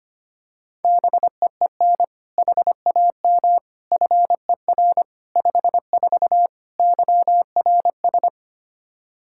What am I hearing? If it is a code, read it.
Morse code, heard as a keyed tone, about 25 wpm, 700 Hz. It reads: BEEN HAM FER 54 YRS
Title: K1USN SST QSO Practice 25wpm